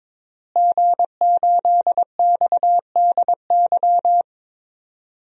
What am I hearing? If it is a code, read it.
Morse code, heard as a keyed tone, about 22 wpm, 700 Hz. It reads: G8XDY
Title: DXCC-100 Call Signs - 5 Characters - Repeated 3x - 22wpm